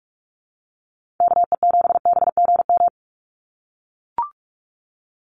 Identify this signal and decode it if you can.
Morse code, heard as a keyed tone, about 44 wpm, 700 Hz. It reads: KE7BGM
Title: NA SOTA Activator Callsigns 45wpm